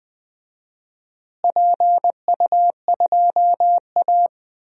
Morse code, heard as a keyed tone, about 20 wpm, 700 Hz. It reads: PU2A
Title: DXCC-100 Call Signs - 4 Characters - Repeated 2x - 20wpm